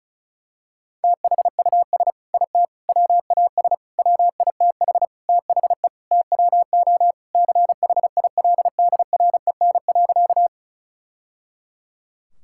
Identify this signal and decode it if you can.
Morse code, heard as a keyed tone, about 34 wpm, 700 Hz. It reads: THUS IT WAS WITH THE TWO CHILDREN.
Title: Sentences from Top 1000 Words - Review Part 5 - 35wpm